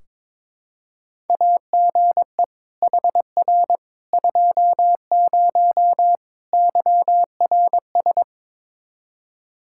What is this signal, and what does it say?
Morse code, heard as a keyed tone, about 22 wpm, 700 Hz. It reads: AGE HR 20 YRS
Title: Practice QSO - Exchange 3 - More Info - 22wpm